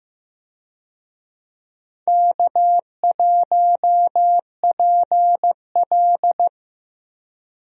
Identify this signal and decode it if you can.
Morse code, heard as a keyed tone, about 15 wpm, 700 Hz. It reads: K1PL
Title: USA Call Signs - 4 Characters - Repeated 2x - 15wpm